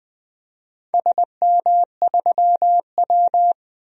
Morse code, heard as a keyed tone, about 20 wpm, 700 Hz. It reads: SM3W
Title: DXCC-100 Call Signs - 4 Characters - 20wpm